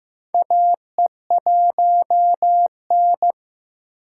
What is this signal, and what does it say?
Morse code, heard as a keyed tone, about 15 wpm, 700 Hz. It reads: AE1N
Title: K1USN SST QSO Practice - 15wpm